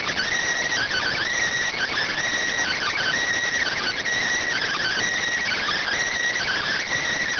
Emergency Alert System (SAME) tones (header bursts), heard as harsh buzzing bursts. It is heard from 0.7 to 6.9 s.